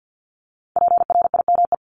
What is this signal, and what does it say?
Morse code, heard as a keyed tone, about 50 wpm, 700 Hz. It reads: PRIME
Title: Multi-letter Phonemes 50wpm